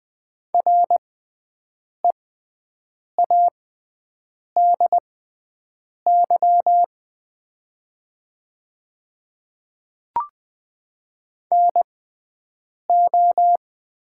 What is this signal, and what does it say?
Morse code, heard as a keyed tone, about 20 wpm, spaced as for 7 wpm, 700 Hz. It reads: READY NO
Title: 6x Character Spacing - Top 500 Words - 20wpm